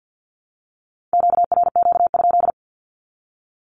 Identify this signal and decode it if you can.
Morse code, heard as a keyed tone, about 52 wpm, 700 Hz. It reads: QRQ?
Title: Q-Codes Repeated 3x 50wpm